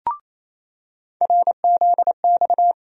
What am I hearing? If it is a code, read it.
Morse code, heard as a keyed tone, about 28 wpm, 700 Hz. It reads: RZX
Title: USA Call Sign Postfixes - 3 Letters 28wpm